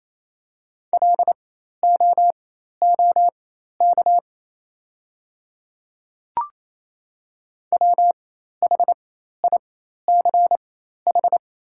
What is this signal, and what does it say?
Morse code, heard as a keyed tone, about 28 wpm, spaced as for 13 wpm, 700 Hz. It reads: LOOK WHICH
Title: 4x Character Spacing - Top 100 Words - 28wpm